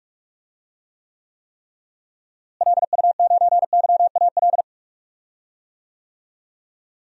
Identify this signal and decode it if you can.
Morse code, heard as a keyed tone, about 44 wpm, 700 Hz. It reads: LU9YAL
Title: DXCC-100 Call Signs - 6 Characters - Repeated 3x - 45wpm